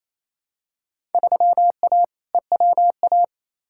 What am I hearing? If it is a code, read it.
Morse code, heard as a keyed tone, about 28 wpm, 700 Hz. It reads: 3A EWA
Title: Field Day Exchanges Encore - 28wpm